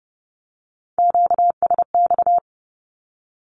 Morse code, heard as a keyed tone, about 30 wpm, 700 Hz. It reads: QSX
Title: Q-Codes Repeated 3x 30wpm